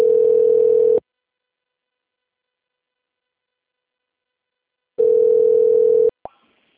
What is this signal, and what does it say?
Ringback tone (North America), 440 + 480 Hz.
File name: rg-1600-unknown- - - .41328